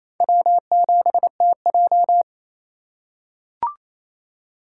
Morse code, heard as a keyed tone, about 28 wpm, 700 Hz. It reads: W7TJ
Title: USA Call Signs - 4 Characters - 4x Character Spacing - 28wpm